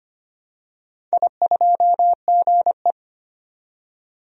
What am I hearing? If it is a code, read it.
Morse code, heard as a keyed tone, about 25 wpm, 700 Hz. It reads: I2GE
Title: DXCC-100 Call Signs - 4 Characters - 25wpm